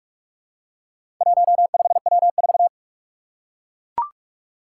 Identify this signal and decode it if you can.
Morse code, heard as a keyed tone, about 44 wpm, 700 Hz. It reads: 15W4
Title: Four Challenging Letter-Number Combinations 45wpm